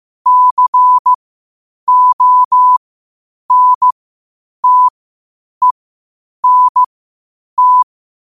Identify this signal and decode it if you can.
Morse code, heard as a keyed tone, about 15 wpm, spaced as for 8 wpm, 1000 Hz. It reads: CONTENT
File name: qotdfiles / 8wpm